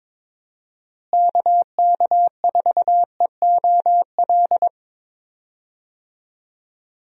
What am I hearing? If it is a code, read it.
Morse code, heard as a keyed tone, about 22 wpm, 700 Hz. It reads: KK4EOL
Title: USA Call Signs - 6 Characters - Repeated 3x - 22wpm